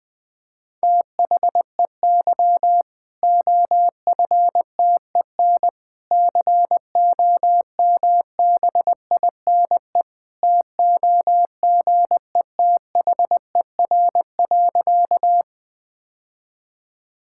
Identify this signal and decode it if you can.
Morse code, heard as a keyed tone, about 20 wpm, 700 Hz. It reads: THEY OFTEN COMBINE TOGETHER.
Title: Sentences from Top 1600 Words 20wpm